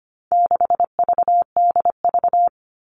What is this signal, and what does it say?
Morse code, heard as a keyed tone, about 25 wpm, 700 Hz. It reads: 6VDV
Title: All Cut Numbers - 4-Digit 25wpm